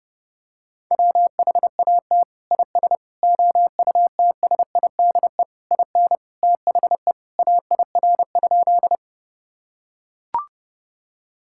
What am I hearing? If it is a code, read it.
Morse code, heard as a keyed tone, about 30 wpm, 700 Hz. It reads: WHAT IS OUTSIDE IN THE AIR?
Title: Sentences from Top 1000 Words - Review Part 6 - 30wpm